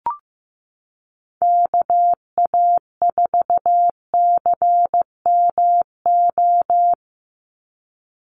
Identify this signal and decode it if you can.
Morse code, heard as a keyed tone, about 15 wpm, 700 Hz. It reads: KA4CMO
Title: USA Call Signs 15wpm